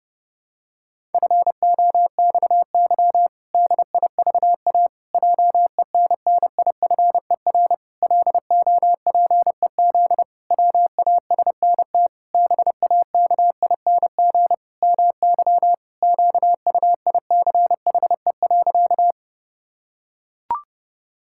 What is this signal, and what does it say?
Morse code, heard as a keyed tone, about 32 wpm, 700 Hz. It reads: FOXY DIVA JENNIFER LOPEZ WASNT BAKING MY QUICHE.
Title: Pangram Challenge v2 - 30wpm